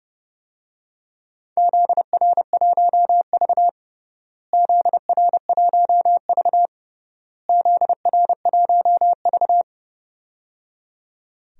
Morse code, heard as a keyed tone, about 30 wpm, 700 Hz. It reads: ZR1V ZR1V ZR1V
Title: DXCC-100 Call Signs - 4 Characters - Repeated 3x - 30wpm